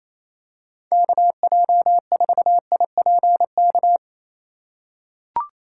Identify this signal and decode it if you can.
Morse code, heard as a keyed tone, about 27 wpm, 700 Hz. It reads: KJ4IPK
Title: USA Call Signs - 6 Characters - 4x Character Spacing - 28wpm